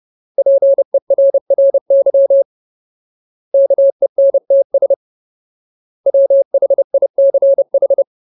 Morse code, heard as a keyed tone, about 30 wpm, 550 Hz. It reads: PERRY KENTS WHICH
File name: Ultimate Word List / 30WPM Ultimate